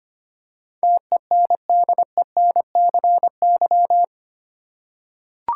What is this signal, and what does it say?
Morse code, heard as a keyed tone, about 25 wpm, 690 Hz. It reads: TENDENCY
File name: 8 Letter Words - First 3 Letters Given - 25wpm